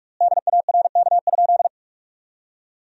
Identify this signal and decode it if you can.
Morse code, heard as a keyed tone, about 41 wpm, 700 Hz. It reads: DARK?
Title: 4x Word-Spacing - Sentences from Top 500 Words - 45wpm